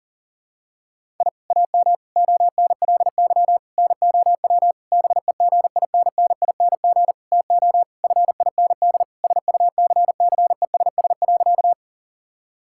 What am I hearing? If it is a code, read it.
Morse code, heard as a keyed tone, about 40 wpm, 700 Hz. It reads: I AM ONLY NOW BEGINNING TO FIND SUCCESS.
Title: Sentences from Top 1000 Words - Review Part 5 - 40wpm